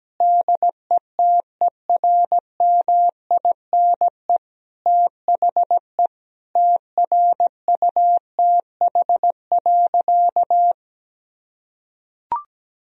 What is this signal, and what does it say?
Morse code, heard as a keyed tone, about 17 wpm, 700 Hz. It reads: DETERMINE THE TRUTH.
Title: Sentences from Top 1000 Words - Review Part 4 - 17wpm